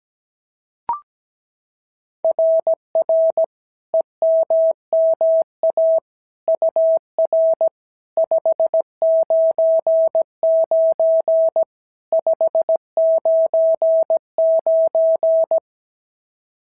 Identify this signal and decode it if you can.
Morse code, heard as a keyed tone, about 16 wpm, 650 Hz. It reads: RR EMMA UR 599 599
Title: SOTA QSO Practice 17wpm